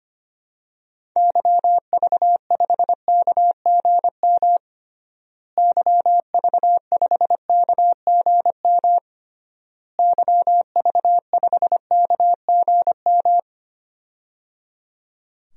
Morse code, heard as a keyed tone, about 25 wpm, 700 Hz. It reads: YV5KGM YV5KGM YV5KGM
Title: DXCC-100 Call Signs - 6 Characters - Repeated 3x - 25wpm